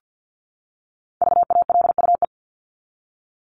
Morse code, heard as a keyed tone, about 50 wpm, 700 Hz. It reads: VALUE
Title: Top 501-750 Words 50wpm